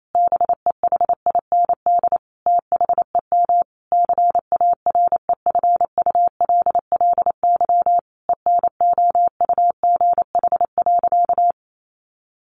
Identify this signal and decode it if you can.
Morse code, heard as a keyed tone, about 28 wpm, 700 Hz. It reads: BEHIND THEM CAREFULLY ENOUGH.
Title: Sentences from Top 1300 Words 28wpm